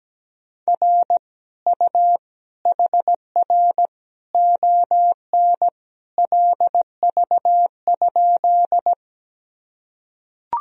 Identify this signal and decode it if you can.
Morse code, heard as a keyed tone, about 17 wpm, 700 Hz. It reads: R U HR ON LV?
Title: Sentences from Top 400 Words and Abbreviations 17wpm